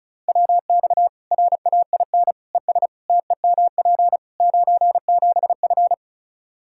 Morse code, heard as a keyed tone, about 34 wpm, 700 Hz. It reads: WX RAIN ES TEMP 97F
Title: Practice QSO - Exchange 2 - Info - 35wpm